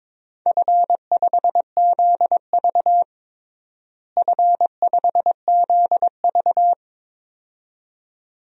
Morse code, heard as a keyed tone, about 23 wpm, 700 Hz. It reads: F5ZV F5ZV
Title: DXCC-100 Call Signs - 4 Characters - Repeated 2x - 22wpm